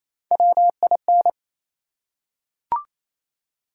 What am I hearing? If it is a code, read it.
Morse code, heard as a keyed tone, about 28 wpm, 700 Hz. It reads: WIN
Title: Three Letter Words - 28wpm